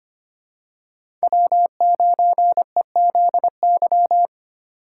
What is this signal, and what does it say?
Morse code, heard as a keyed tone, about 25 wpm, 700 Hz. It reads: W9EZY
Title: USA Call Signs - 5 Characters - 4x Character Spacing - 25wpm